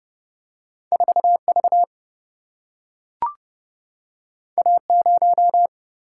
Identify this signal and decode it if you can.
Morse code, heard as a keyed tone, about 30 wpm, 700 Hz. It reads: 4V A0
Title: Single-Letter Single-Number 30wpm